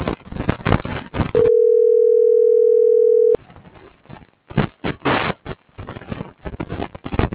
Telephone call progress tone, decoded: ringback tone

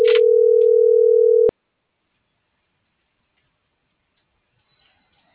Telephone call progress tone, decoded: ringback tone

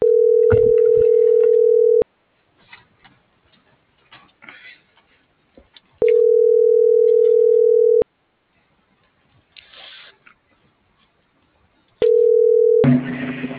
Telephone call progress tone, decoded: ringback tone

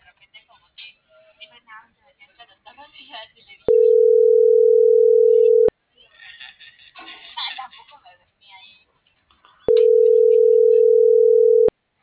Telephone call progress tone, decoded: ringback tone

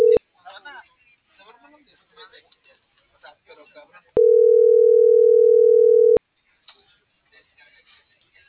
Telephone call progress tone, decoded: ringback tone

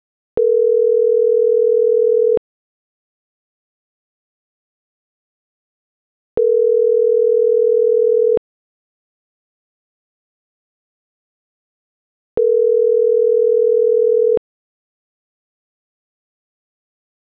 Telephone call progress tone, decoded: ringback tone

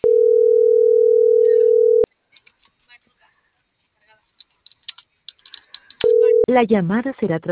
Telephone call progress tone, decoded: ringback tone

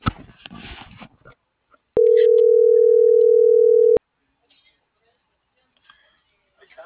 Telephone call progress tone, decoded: ringback tone

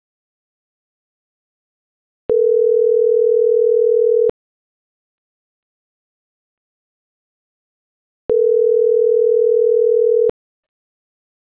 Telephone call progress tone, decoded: ringback tone

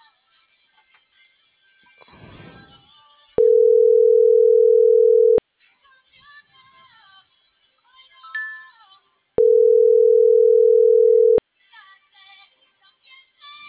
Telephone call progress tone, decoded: ringback tone

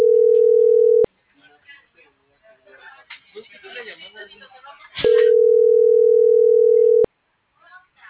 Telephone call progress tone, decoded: ringback tone

